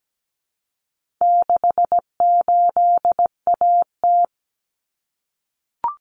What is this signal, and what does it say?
Morse code, heard as a keyed tone, about 17 wpm, 700 Hz. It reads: 68AT
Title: Common Cut Numbers - 4-Digit 17wpm